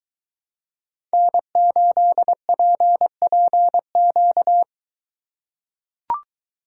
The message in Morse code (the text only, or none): N8PPQ